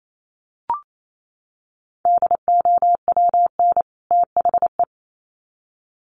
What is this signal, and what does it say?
Morse code, heard as a keyed tone, about 28 wpm, 700 Hz. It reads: DOWN THE
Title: Sets of 2 Words 28wpm